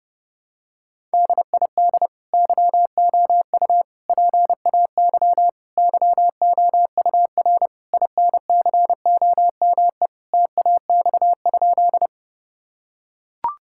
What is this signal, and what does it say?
Morse code, heard as a keyed tone, about 30 wpm, 700 Hz. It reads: DID YOU PAY YOUR INCOME TAX?
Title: Sentences from Top 1000 Words - Review Part 5 - 30wpm